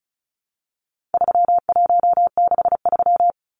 Morse code, heard as a keyed tone, about 34 wpm, 700 Hz. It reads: 3163
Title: Numbers - 4-Digit 35wpm